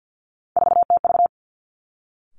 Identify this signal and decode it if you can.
Morse code, heard as a keyed tone, about 50 wpm, 700 Hz. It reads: 4TV
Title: Three Challenging Letter-Number Combinations 50wpm